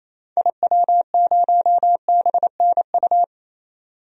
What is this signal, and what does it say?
Morse code, heard as a keyed tone, about 28 wpm, 700 Hz. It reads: IW0BNU
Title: DXCC-100 Call Signs - 6 Characters - 28wpm